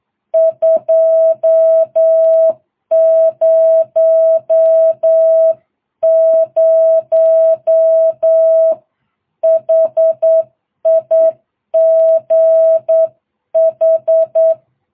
Morse code, heard as a keyed tone, about 8 wpm, 660 Hz. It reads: 200 HIGH